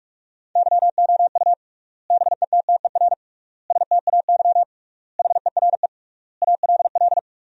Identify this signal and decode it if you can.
Morse code, heard as a keyed tone, about 46 wpm, 700 Hz. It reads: YOU BETTER STAY HERE ALL
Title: 3x Word-Spacing - Sentences from Top 500 Words - 45wpm